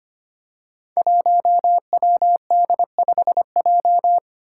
Morse code, heard as a keyed tone, about 25 wpm, 700 Hz. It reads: 1WD5J
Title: Five Challenging Letter-Number Combinations 25wpm